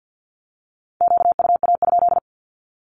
Morse code, heard as a keyed tone, about 50 wpm, 700 Hz. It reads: QUA?
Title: Q-Codes 50wpm